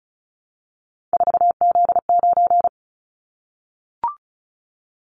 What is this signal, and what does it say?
Morse code, heard as a keyed tone, about 36 wpm, 700 Hz. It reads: 4Z9
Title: Callsign Prefixes for Top 50 DXCC - 35wpm